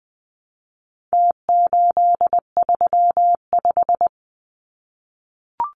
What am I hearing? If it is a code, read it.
Morse code, heard as a keyed tone, about 20 wpm, 700 Hz. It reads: T835
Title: Numbers - Contest Style - 4-Digit 20wpm